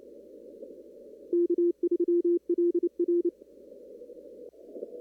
Morse code, heard as a keyed tone, about 28 wpm, 350 Hz. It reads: K3LR
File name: CW / kt5j wpx cw-op-k5pi